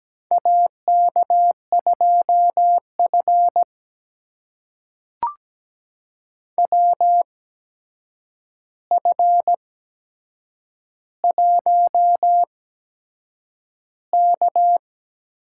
Morse code, heard as a keyed tone, about 17 wpm, spaced as for 7 wpm, 700 Hz. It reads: AK2F WF1K
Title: USA Call Signs - 4 Characters - 8x Character Spacing - 17wpm